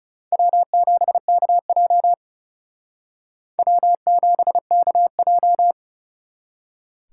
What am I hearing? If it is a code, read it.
Morse code, heard as a keyed tone, about 36 wpm, 700 Hz. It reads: W7KJ W7KJ
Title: Speed Racing - USA 1x2 Call Signs 30wpm